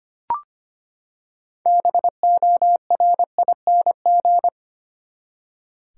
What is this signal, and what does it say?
Morse code, heard as a keyed tone, about 25 wpm, 700 Hz. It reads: BORING
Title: Six Letter Words - 25wpm